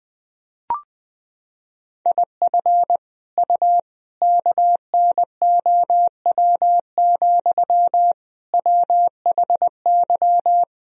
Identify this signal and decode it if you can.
Morse code, heard as a keyed tone, about 20 wpm, 700 Hz. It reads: IF U KNOW, WHY
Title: Sentences from Top 300 Words and Abbreviations 20wpm